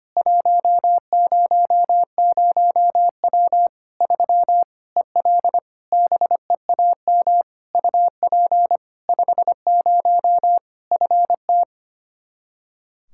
Morse code, heard as a keyed tone, about 25 wpm, 700 Hz. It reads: 100W 3 EL BEAM UP 50 FT
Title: Practice QSO - Exchange 2 - Info - 25wpm